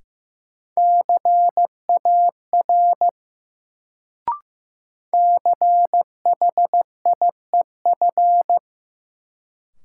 Morse code, heard as a keyed tone, about 16 wpm, 700 Hz. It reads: CAR CHIEF
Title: Sets of 5 Words Encore - New Word Drill - 15wpm